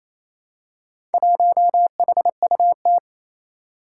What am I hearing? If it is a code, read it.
Morse code, heard as a keyed tone, about 27 wpm, 700 Hz. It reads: 1HUT
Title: Four Challenging Letter-Number Combinations 28wpm